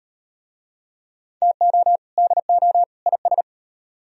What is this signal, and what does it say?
Morse code, heard as a keyed tone, about 34 wpm, 700 Hz. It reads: TO DO IS